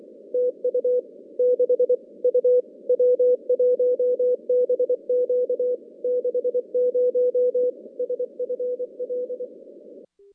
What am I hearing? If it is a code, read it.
Morse code, heard as a keyed tone, about 24 wpm, 500 Hz. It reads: TU 6 U W1BQ 60 SFL
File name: contests / CW-SS-k5tr- -2100